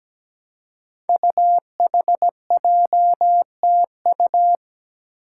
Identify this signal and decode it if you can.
Morse code, heard as a keyed tone, about 17 wpm, 700 Hz. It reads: UHJTU
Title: Five Challenging Letter-Number Combinations 17wpm